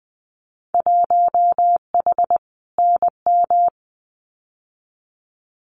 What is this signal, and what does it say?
Morse code, heard as a keyed tone, about 20 wpm, 700 Hz. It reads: 1H NM
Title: Winter Field Day Exchanges 20wpm